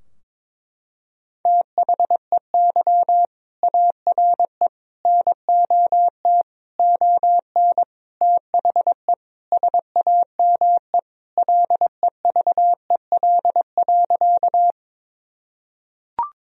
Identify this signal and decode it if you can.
Morse code, heard as a keyed tone, about 21 wpm, 700 Hz. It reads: THEY ARE NOT ON THE SAME LEVEL.
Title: Sentences from Top 1000 Words - Review Part 4 - 22wpm